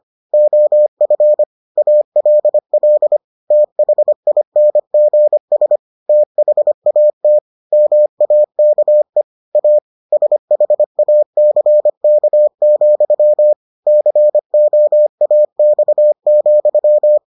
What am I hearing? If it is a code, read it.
Morse code, heard as a keyed tone, about 25 wpm, 600 Hz. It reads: OF ALL THINGS THAT MAKE A SHACK, COAX,